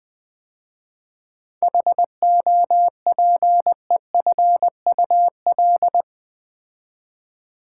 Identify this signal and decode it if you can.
Morse code, heard as a keyed tone, about 20 wpm, 700 Hz. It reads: HOPEFUL
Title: 7 Letter Words - First 3 Letters Given - 20wpm